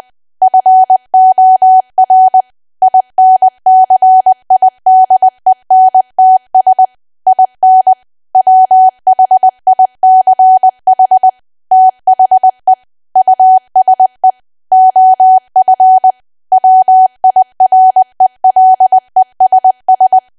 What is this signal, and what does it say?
Morse code, heard as a keyed tone, about 20 wpm, 750 Hz. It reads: FOR INCIDENTS IN WHICH THE USE OF WIRELESS
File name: cwa / W1AW-Level3 / 20WPM